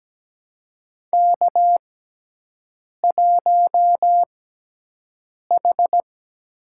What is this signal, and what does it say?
Morse code, heard as a keyed tone, about 17 wpm, spaced as for 6 wpm, 700 Hz. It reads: K1H
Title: USA Call Signs - 4 Characters - 6x Character Spacing - 17wpm